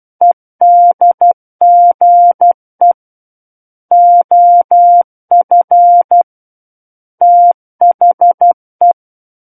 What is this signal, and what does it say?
Morse code, heard as a keyed tone, about 12 wpm, 700 Hz. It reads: EDGE OF THE
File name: Book One / 03-Chapter 3 - 12 WPM